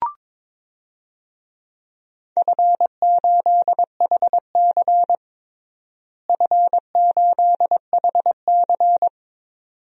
Morse code, heard as a keyed tone, about 22 wpm, 700 Hz. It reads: F8HC F8HC
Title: DXCC-100 Call Signs - 4 Characters - Repeated 2x - 22wpm